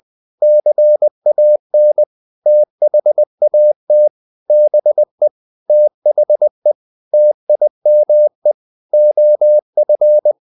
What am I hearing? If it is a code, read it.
Morse code, heard as a keyed tone, about 20 wpm, 600 Hz. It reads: CAN THAT BE THE TIME OF
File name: the old man and the sea / the old man and the sea part 1 20wpm